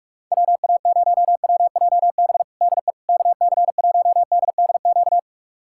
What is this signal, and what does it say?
Morse code, heard as a keyed tone, about 41 wpm, 700 Hz. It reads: WA0WJB DE KC1DDQ
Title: USA Callsign DE Callsign - 6 Characters - 45wpm